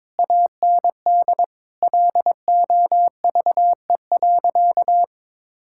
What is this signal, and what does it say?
Morse code, heard as a keyed tone, about 22 wpm, 700 Hz. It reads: AND LOVE.
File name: Difficult Sentences with Hints - 22wpm